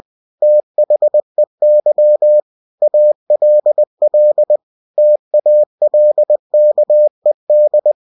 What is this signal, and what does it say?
Morse code, heard as a keyed tone, about 20 wpm, 600 Hz. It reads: THEY ALL TALKED
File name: cw-books / winnie-the-pooh / winnie-20wpm-page-0033